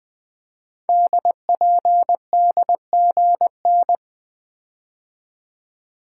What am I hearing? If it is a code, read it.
Morse code, heard as a keyed tone, about 20 wpm, 700 Hz. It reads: DPDGN
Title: Code Groups - 5 Letters - 20wpm